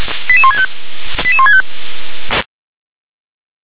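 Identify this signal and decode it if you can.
Five-tone selective call (ZVEI), heard as quick runs of five tones, tones of about 70 ms. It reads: 88165 88165
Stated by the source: ZVEI five-tone selective call received by SDR.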